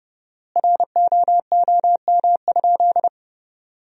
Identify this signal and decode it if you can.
Morse code, heard as a keyed tone, about 30 wpm, 700 Hz. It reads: ROOM?
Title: 4x Word-Spacing - Sentences from Top 300 Words - 30wpm